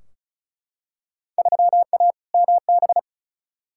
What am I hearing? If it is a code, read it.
Morse code, heard as a keyed tone, about 36 wpm, 690 Hz. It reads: 3A MB
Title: Field Day Exchanges Encore - 35wpm